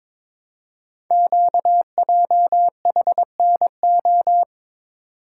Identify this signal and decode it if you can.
Morse code, heard as a keyed tone, about 22 wpm, 700 Hz. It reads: QJHNO
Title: Code Groups - 5 Letters - 22wpm